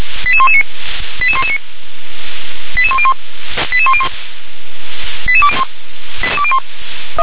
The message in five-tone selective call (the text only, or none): 88119 88119 88101 88101 88201